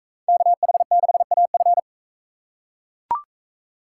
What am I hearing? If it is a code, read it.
Morse code, heard as a keyed tone, about 43 wpm, 700 Hz. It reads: KH6AF